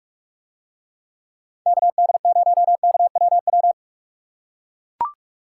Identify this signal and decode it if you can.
Morse code, heard as a keyed tone, about 43 wpm, 700 Hz. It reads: KD0KWW